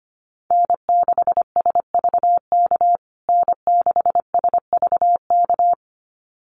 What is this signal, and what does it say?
Morse code, heard as a keyed tone, about 25 wpm, 700 Hz. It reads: N6SVK N6SVK
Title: USA Call Signs Repeated 2x 25wpm